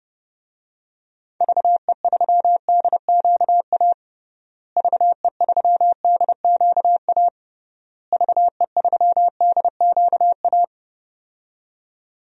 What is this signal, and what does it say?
Morse code, heard as a keyed tone, about 32 wpm, 700 Hz. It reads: VE3DQA VE3DQA VE3DQA
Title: DXCC-100 Call Signs - 6 Characters - Repeated 3x - 30wpm